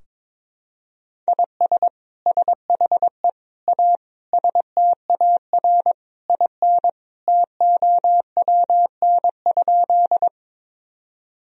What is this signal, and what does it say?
Morse code, heard as a keyed tone, about 22 wpm, 700 Hz. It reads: IS SHE A STAR IN TOWN?
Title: Sentences from Top 1000 Words - Review Part 2 - 22wpm